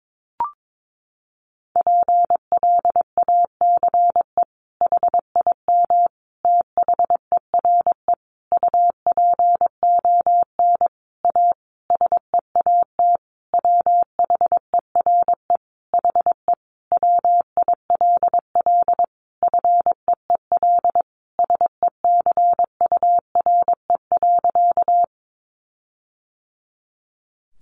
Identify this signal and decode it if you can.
Morse code, heard as a keyed tone, about 22 wpm, 700 Hz. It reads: PLACE HIM THERE UPON A SEAT WHERE HE WILL FEEL SECURE.
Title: Sentences from Top 1500 Words 22wpm